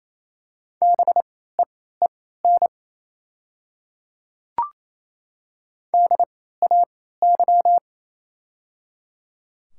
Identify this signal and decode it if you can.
Morse code, heard as a keyed tone, about 28 wpm, spaced as for 16 wpm, 700 Hz. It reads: BEEN DAY
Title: 3x Character Spacing - Top 100 Words - 28wpm